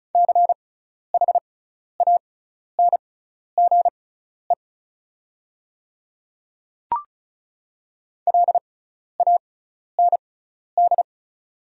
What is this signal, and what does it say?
Morse code, heard as a keyed tone, about 36 wpm, spaced as for 12 wpm, 700 Hz. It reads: CHANGE LAND
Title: 6x Character Spacing - Top 200 Words - 35wpm